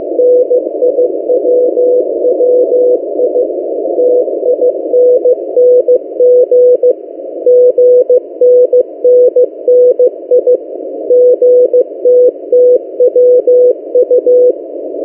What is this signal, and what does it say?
Morse code, heard as a keyed tone, about 15 wpm, 500 Hz. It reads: NIWWI TINNG GNNNI GTTWU
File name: M08a- -1355utc-8096khz